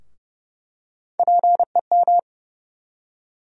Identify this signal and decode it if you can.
Morse code, heard as a keyed tone, about 32 wpm, 700 Hz. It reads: PEM
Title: USA Call Sign Postfixes - 3 Letters 30wpm